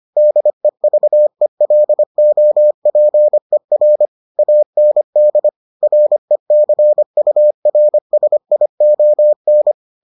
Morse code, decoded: DEVELOPER AND RECURSION

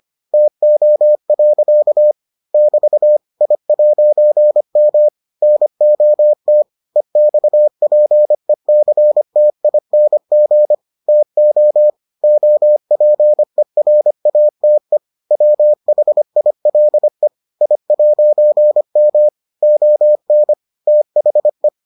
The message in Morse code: TO. <BT> I'M NOT EXPECTING TO OPERATE WHILE I'M ON THE

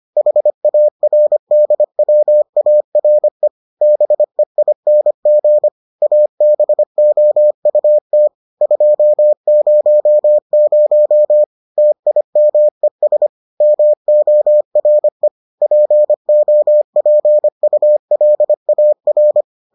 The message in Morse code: HARDWARE BEING ABOUT 200 TIMES MORE POPULAR

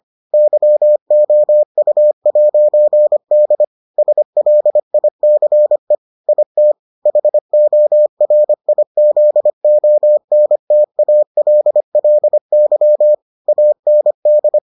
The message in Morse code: YOU'D SLICE IT HORIZONTALLY AND

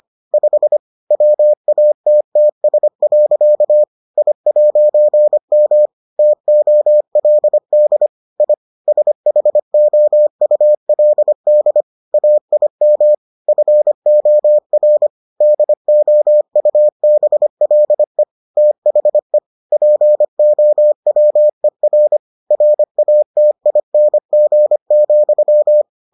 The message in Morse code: 5 WATTS. I'M TOLD I SHOULD AIM FOR DOUBLE THE POWER RATING,

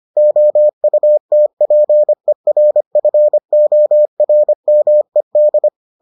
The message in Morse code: OUTPERFORMED

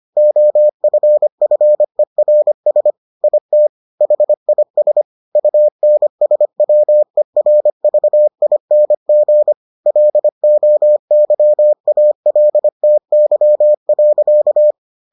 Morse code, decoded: OFFERS IT HIS UNSWERVING LOYALTY.

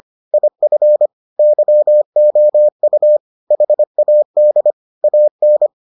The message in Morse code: IF YOU HAD AN

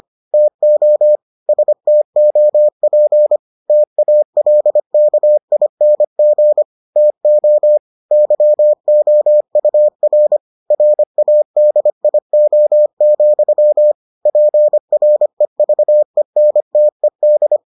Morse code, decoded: TO STOP TALKING TO YOUR RADIO, PREVENTED

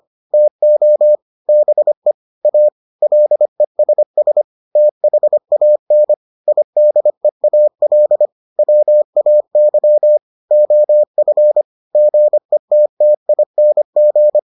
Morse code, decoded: TO BE A LESS THAN IDEAL WAY OF GETTING

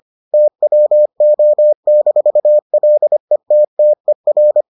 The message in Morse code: TWO-LETTER

